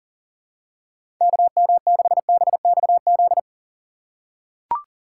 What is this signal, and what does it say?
Morse code, heard as a keyed tone, about 40 wpm, 700 Hz. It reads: KM6BXZ
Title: USA Call Signs - 4x Character Spacing - 40wpm